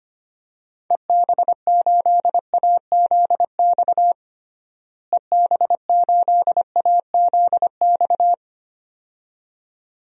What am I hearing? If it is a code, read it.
Morse code, heard as a keyed tone, about 25 wpm, 700 Hz. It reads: EB8AZX EB8AZX
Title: DXCC-100 Call Signs - 6 Characters - Repeated 2x - 25wpm